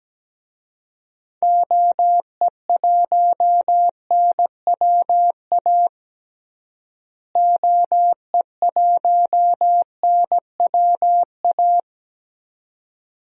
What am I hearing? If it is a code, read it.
Morse code, heard as a keyed tone, about 17 wpm, 700 Hz. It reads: OE1NWA OE1NWA
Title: DXCC-100 Call Signs - 6 Characters - Repeated 2x - 17wpm